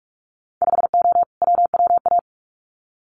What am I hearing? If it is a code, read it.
Morse code, heard as a keyed tone, about 46 wpm, 700 Hz. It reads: 5O WWA
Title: Winter Field Day Exchanges Encore 45wpm